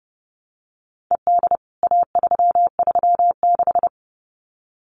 Morse code, heard as a keyed tone, about 30 wpm, 700 Hz. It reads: ED A336
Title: CWT Exchange 30wpm